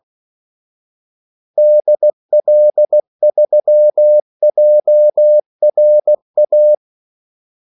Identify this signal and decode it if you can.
Morse code, heard as a keyed tone, about 16 wpm, 600 Hz. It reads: DL3JRA
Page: Kaldesignaler 16 wpm